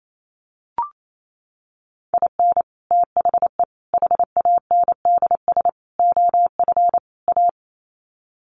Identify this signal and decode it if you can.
Morse code, heard as a keyed tone, about 28 wpm, 700 Hz. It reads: IN THE HANDS OF A
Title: Sets of 5 Words Encore 28wpm